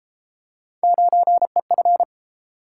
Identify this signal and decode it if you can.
Morse code, heard as a keyed tone, about 32 wpm, 700 Hz. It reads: 9EF